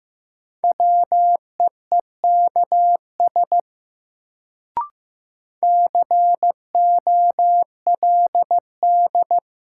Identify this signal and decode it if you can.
Morse code, heard as a keyed tone, about 15 wpm, 700 Hz. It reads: WEEKS COLD
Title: Sets of 5 Words - New Word Drill - 15wpm